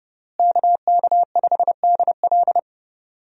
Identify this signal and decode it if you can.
Morse code, heard as a keyed tone, about 32 wpm, 700 Hz. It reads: KK5DL
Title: USA Call Signs - 5 Characters - Repeated 3x - 30wpm